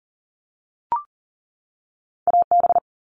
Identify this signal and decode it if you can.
Morse code, heard as a keyed tone, about 40 wpm, 700 Hz. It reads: AB
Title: ARRL Sections 40wpm